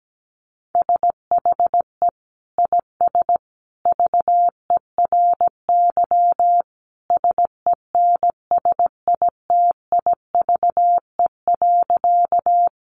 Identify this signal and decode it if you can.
Morse code, heard as a keyed tone, about 17 wpm, 700 Hz. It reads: SHE IS VERY SENSITIVE.
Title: Sentences from Top 1400 Words 17wpm